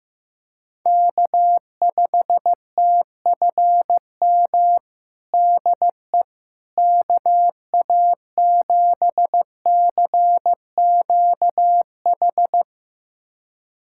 Morse code, heard as a keyed tone, about 15 wpm, 700 Hz. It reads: K5TFM DE KA7CQH
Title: USA Callsign DE Callsign 15wpm